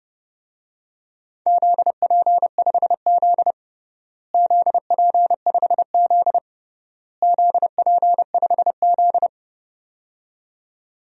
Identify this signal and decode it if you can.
Morse code, heard as a keyed tone, about 30 wpm, 700 Hz. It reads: ZP5Z ZP5Z ZP5Z
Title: DXCC-100 Call Signs - 4 Characters - Repeated 3x - 30wpm